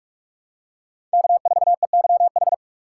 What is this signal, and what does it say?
Morse code, heard as a keyed tone, about 46 wpm, 700 Hz. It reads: K4EYH
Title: USA Call Signs - 5 Characters - 8x Character Spacing - 45wpm